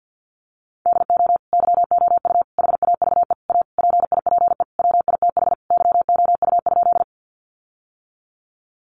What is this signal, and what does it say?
Morse code, heard as a keyed tone, about 50 wpm, 700 Hz. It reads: DO YOU HAVE A PIPE WITH YOU?
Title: Sentences from Top 1500 Words 50wpm